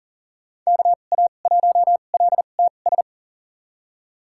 Morse code, heard as a keyed tone, about 40 wpm, spaced as for 29 wpm, 700 Hz. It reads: KA1LTS
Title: USA Call Signs - 2x Character Spacing - 40wpm